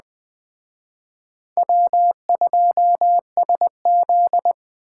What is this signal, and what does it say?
Morse code, heard as a keyed tone, about 20 wpm, 700 Hz. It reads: W2SZ
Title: Speed Racing - USA 1x2 Call Signs 17wpm